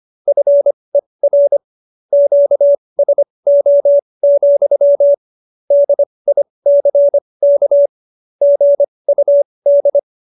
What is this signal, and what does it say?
Morse code, heard as a keyed tone, about 25 wpm, spaced as for 20 wpm, 570 Hz. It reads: FER QSO, DICK GUD
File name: QSO302 20